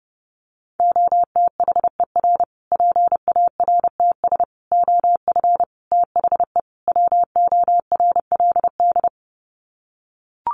Morse code, heard as a keyed tone, about 30 wpm, 700 Hz. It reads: OTHER PARTS OF THE WORLD
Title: Sets of 5 Words Encore 30wpm